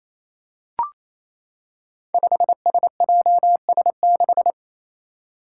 Morse code, heard as a keyed tone, about 27 wpm, 700 Hz. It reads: 5SJS6
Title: Five Challenging Letter-Number Combinations 28wpm